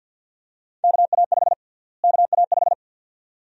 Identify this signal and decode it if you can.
Morse code, heard as a keyed tone, about 47 wpm, 700 Hz. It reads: KA5 KA5
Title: USA Call Sign Prefixes - Repeated 2x - 50wpm